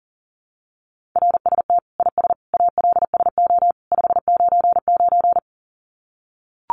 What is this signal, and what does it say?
Morse code, heard as a keyed tone, about 40 wpm, 700 Hz. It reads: RST IS ALSO 599
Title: Common QSO Phrases 40wpm